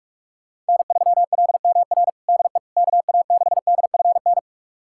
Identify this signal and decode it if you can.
Morse code, heard as a keyed tone, about 45 wpm, 700 Hz. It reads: N3LMR DE KA6DFN
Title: USA Callsign DE Callsign 45wpm